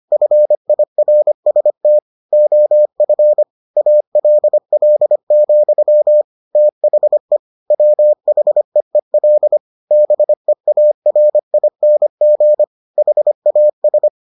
Morse code, heard as a keyed tone, about 25 wpm, 600 Hz. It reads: FIRST OF ALL, THE WHEEL BEARING HAS